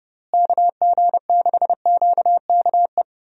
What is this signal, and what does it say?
Morse code, heard as a keyed tone, about 30 wpm, 700 Hz. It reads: KG6QKE
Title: USA Call Signs - 6 Characters - Repeated 2x - 30wpm